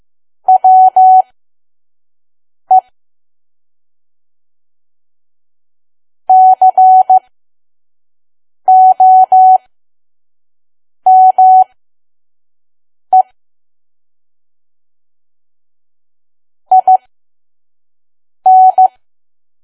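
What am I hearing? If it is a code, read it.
Morse code, heard as a keyed tone, about 15 wpm, spaced as for 5 wpm, 750 Hz. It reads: WE COME IN
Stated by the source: Morse code practice at 5 WPM from KQ2P.COM.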